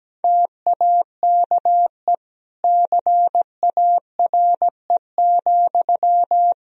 Morse code, heard as a keyed tone, about 17 wpm, 700 Hz. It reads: TAKE CARE,
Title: Sentences from Top 1000 Words - Review Part 4 - 17wpm